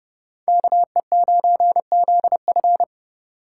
Morse code, heard as a keyed tone, about 38 wpm, 700 Hz. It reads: KE9ZF